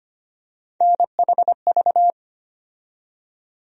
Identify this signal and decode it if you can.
Morse code, heard as a keyed tone, about 25 wpm, 700 Hz. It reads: NHV
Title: USA Call Sign Postfixes - 3 Letters 25wpm